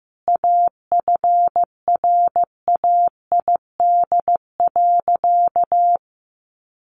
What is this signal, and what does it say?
Morse code, heard as a keyed tone, about 15 wpm, 700 Hz. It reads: AFRAID.
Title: Sentences from Top 1500 Words 15wpm